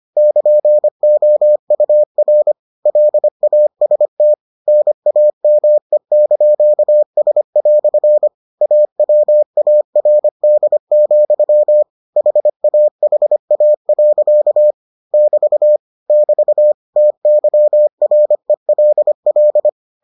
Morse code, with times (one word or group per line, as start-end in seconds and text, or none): 0.2-2.5 <KN>OUR
2.9-4.3 LAST
4.7-8.3 NAME)S"
8.6-11.8 AWARD,
12.2-14.7 HAHA.
15.1-15.8 <BT>
16.1-16.7 <BT>
17.0-19.7 TYRELL